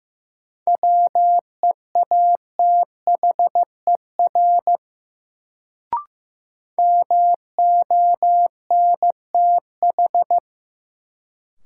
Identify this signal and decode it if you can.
Morse code, heard as a keyed tone, about 15 wpm, 700 Hz. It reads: WEATHER MONTH
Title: Sentences from Top 800 Words - New Word Drill - 15wpm